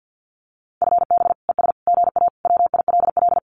Morse code, heard as a keyed tone, about 48 wpm, 700 Hz. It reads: FB ES GA WILL
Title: Common QSO Phrases Encore 50wpm